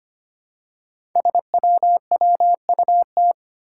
Morse code, heard as a keyed tone, about 25 wpm, 700 Hz. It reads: SWWUT
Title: Five Challenging Letter-Number Combinations 25wpm